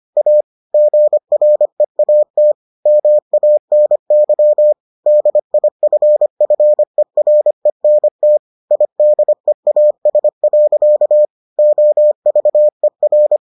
Morse code, heard as a keyed tone, about 25 wpm, 600 Hz. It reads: A GREAT MANY DIFFERENT IDEAS. OVER